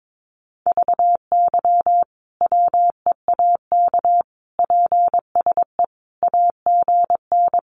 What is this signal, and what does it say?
Morse code, heard as a keyed tone, about 22 wpm, 700 Hz. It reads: VY WEAK PSE AGN
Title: Common QSO Phrases 22wpm